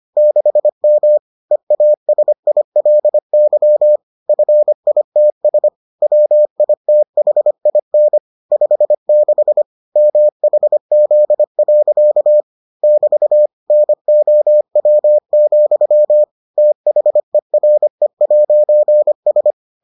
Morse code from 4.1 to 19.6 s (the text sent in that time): FITS WITHIN 56 MHZ. <BT> NOW, THERE'S